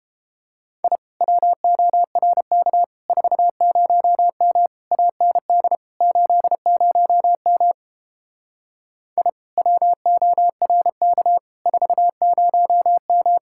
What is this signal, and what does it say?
Morse code, heard as a keyed tone, about 31 wpm, 700 Hz. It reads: I WORK 40M AND 80M I WORK 40M